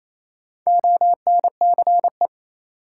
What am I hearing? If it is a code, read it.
Morse code, heard as a keyed tone, about 28 wpm, 700 Hz. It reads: ONCE
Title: Sentences from Top 300 Words - New Word Drill - 28wpm